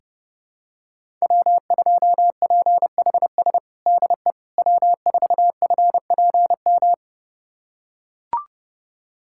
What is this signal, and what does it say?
Morse code, heard as a keyed tone, about 30 wpm, 700 Hz. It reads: W2PHS DE W4FPM
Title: USA Callsign DE Callsign - 5 Characters - 30wpm